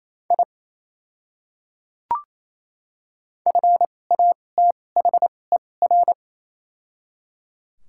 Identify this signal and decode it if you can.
Morse code, heard as a keyed tone, about 28 wpm, spaced as for 20 wpm, 700 Hz. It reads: I FATHER
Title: 2x Character Spacing - Top 300 Words - 28wpm